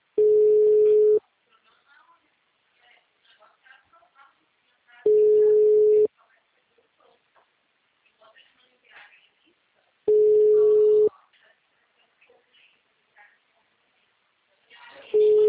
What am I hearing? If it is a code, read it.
Ringback tone (Europe, 425 Hz), with speech.